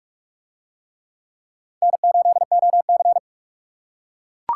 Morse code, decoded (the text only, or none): N8OC